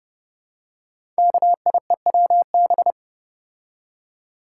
Morse code, heard as a keyed tone, about 30 wpm, 700 Hz. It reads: KIEWB
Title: Code Groups - 5 Letters - 30wpm